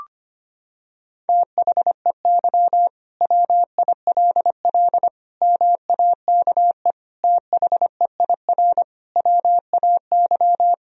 Morse code, heard as a keyed tone, about 25 wpm, 700 Hz. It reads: THEY WILL MAKE THEIR WAY
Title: Sentences from Top 1000 Words - Review Part 3 - 25wpm